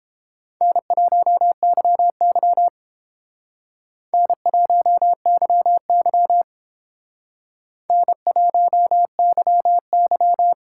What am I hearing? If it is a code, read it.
Morse code, heard as a keyed tone, about 26 wpm, 700 Hz. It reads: N1YY N1YY N1YY
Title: Speed Racing - USA 1x2 Call Signs 22wpm